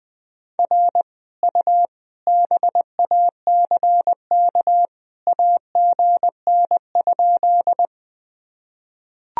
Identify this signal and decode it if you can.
Morse code, heard as a keyed tone, about 21 wpm, 700 Hz. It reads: R U BACK AGN?
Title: Sentences from Top 200 Words and Abbreviations 20wpm